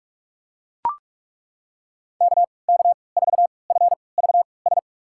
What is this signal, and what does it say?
Morse code, heard as a keyed tone, about 42 wpm, spaced as for 25 wpm, 700 Hz. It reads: KK4FVS
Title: USA Call Signs - 3x Character Spacing - 45wpm